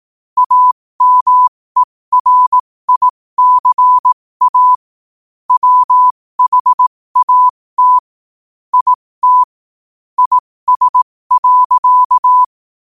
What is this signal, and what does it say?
Morse code, heard as a keyed tone, about 18 wpm, spaced as for 15 wpm, 1000 Hz. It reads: AMERICA WHAT IT IS.
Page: qotd-at-15-wpm